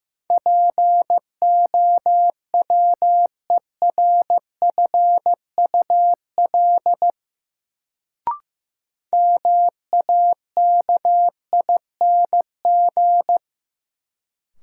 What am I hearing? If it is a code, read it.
Morse code, heard as a keyed tone, about 15 wpm, 700 Hz. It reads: POWERFUL MAKING
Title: Sentences from Top 1000 Words - Review Part 6 - New Word Drill - 15wpm